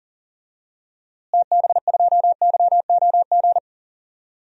Morse code, heard as a keyed tone, about 40 wpm, 700 Hz. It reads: TB2YOG
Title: DXCC-100 Call Signs - 6 Characters - Repeated 2x - 40wpm